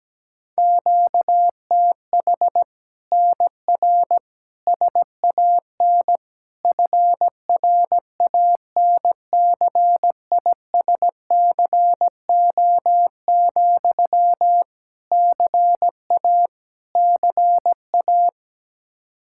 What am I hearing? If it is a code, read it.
Morse code, heard as a keyed tone, about 17 wpm, 700 Hz. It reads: QTH NR SAN FRANCISCO, CA CA
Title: Practice QSO - Exchange 1 - Beginning - 17wpm